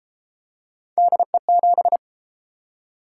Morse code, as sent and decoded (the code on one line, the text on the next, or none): -.. --...
DE7